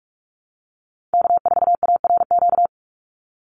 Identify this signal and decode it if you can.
Morse code, heard as a keyed tone, about 44 wpm, 700 Hz. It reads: K4ARQ
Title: Call Signs Repeated 3x - 45wpm